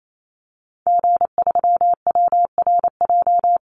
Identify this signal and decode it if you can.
Morse code, heard as a keyed tone, about 28 wpm, 700 Hz. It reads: G3WRJ
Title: Call Signs - 28wpm